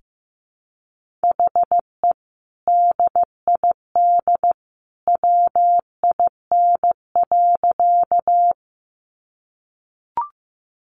Morse code, heard as a keyed tone, about 15 wpm, 700 Hz. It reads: HE DID WIN.
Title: Sentences from Top 1500 Words 15wpm